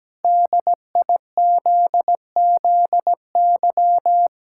Morse code, heard as a keyed tone, about 17 wpm, 700 Hz. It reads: DIZZY
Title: Pangram Challenge v2 - 17wpm